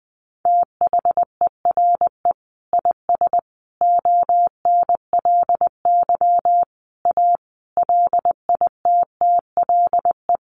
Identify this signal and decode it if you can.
Morse code, heard as a keyed tone, about 20 wpm, 700 Hz. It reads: THERE IS ONLY A LITTLE
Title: Sentences from Top 1500 Words 20wpm